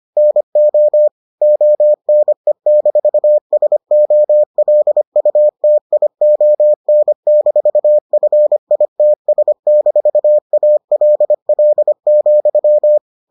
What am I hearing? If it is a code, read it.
Morse code, heard as a keyed tone, about 25 wpm, 600 Hz. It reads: NO ONE-SOLUTION-FITS-ALL,